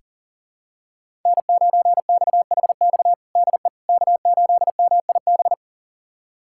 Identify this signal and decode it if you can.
Morse code, heard as a keyed tone, about 40 wpm, 700 Hz. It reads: N9XHX DE K8MIB
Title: USA Callsign DE Callsign - 5 Characters - 40wpm